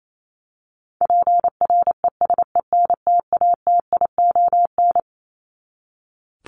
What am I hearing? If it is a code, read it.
Morse code, heard as a keyed tone, about 28 wpm, 700 Hz. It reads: PRESENTATION
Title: Top 1501-1750 Words 28wpm